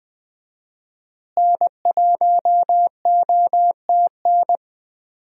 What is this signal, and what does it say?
Morse code, heard as a keyed tone, about 20 wpm, 700 Hz. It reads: N1OTN
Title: USA Call Signs - 5 Characters - Repeated 2x - 20wpm